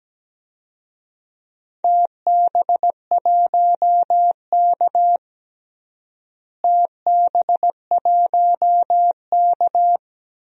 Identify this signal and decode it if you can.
Morse code, heard as a keyed tone, about 17 wpm, 700 Hz. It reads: TB1K TB1K
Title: DXCC-100 Call Signs - 4 Characters - Repeated 2x - 17wpm